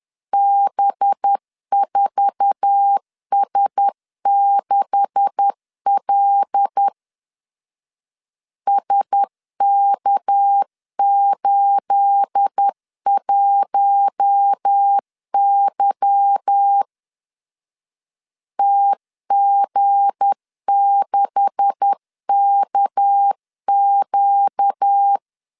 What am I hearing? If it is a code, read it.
Morse code, heard as a keyed tone, about 10 wpm, 800 Hz. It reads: B4S6L SK81Y TG6KQ
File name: LEZIONI / 9 286 10wpm 2735